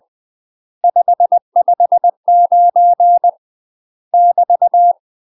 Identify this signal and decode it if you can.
Morse code, heard as a keyed tone, about 20 wpm, 700 Hz. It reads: 559 <BT>
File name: UK MASTER QSOs TRAINING / 20WPM / UK MASTER QSOs 0043